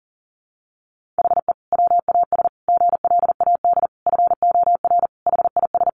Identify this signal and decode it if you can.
Morse code, heard as a keyed tone, about 37 wpm, 700 Hz. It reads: HE WAS GLAD FOR HIS
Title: Sentences from Top 1500 Words 40wpm